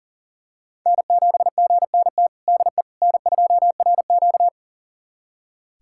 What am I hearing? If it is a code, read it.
Morse code, heard as a keyed tone, about 37 wpm, 700 Hz. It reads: N7GNT DE N2RQ
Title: USA Callsign DE Callsign 40wpm